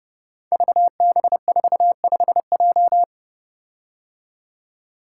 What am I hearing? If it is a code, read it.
Morse code, heard as a keyed tone, about 30 wpm, 700 Hz. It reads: VB45J
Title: Code Groups - 5 Letter:Numbers - 30wpm